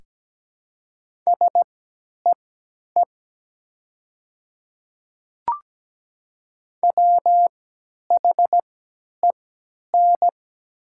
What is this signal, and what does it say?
Morse code, heard as a keyed tone, about 17 wpm, spaced as for 10 wpm, 700 Hz. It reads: SEE WHEN
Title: 3x Character Spacing - Top 200 Words - 17wpm